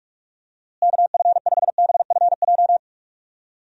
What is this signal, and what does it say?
Morse code, heard as a keyed tone, about 41 wpm, 700 Hz. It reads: KF5BFJ
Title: USA Call Signs - 6 Characters - Repeated 3x - 45wpm